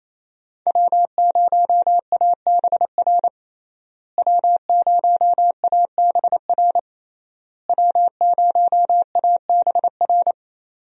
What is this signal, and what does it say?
Morse code, heard as a keyed tone, about 28 wpm, 700 Hz. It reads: W0ABR W0ABR W0ABR
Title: USA Call Signs - 5 Characters - Repeated 3x - 28wpm